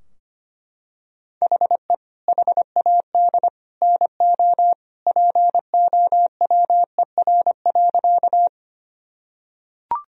Morse code, heard as a keyed tone, about 25 wpm, 700 Hz. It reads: HE HAD NO POWER.
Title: Sentences from Top 1000 Words - Review Part 2 - 25wpm